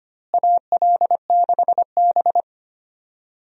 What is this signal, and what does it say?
Morse code, heard as a keyed tone, about 25 wpm, 700 Hz. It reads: AL6B
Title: USA Call Signs - 4 Characters - Repeated 3x - 25wpm